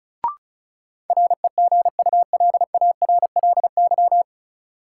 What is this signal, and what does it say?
Morse code, heard as a keyed tone, about 36 wpm, 700 Hz. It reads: REGULARLY
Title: Sentences from Top 1400 Words - New Word Drill - 35wpm